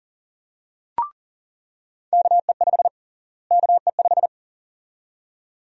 Morse code, heard as a keyed tone, about 40 wpm, 700 Hz. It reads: KE5 KE5
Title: USA Call Sign Prefixes - Repeated 2x - 40wpm